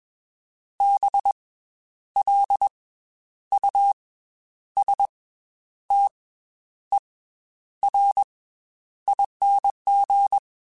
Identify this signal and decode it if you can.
Morse code, heard as a keyed tone, about 21 wpm, 800 Hz. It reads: B L U S T E R ING